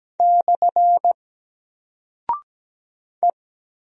Morse code, heard as a keyed tone, about 17 wpm, 700 Hz. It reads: / E
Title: Single Character (Letter:Number:Punctuation:Prosign) Alt 17wpm